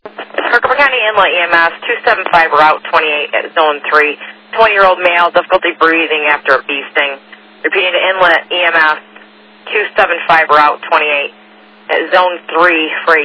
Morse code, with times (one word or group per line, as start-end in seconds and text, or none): none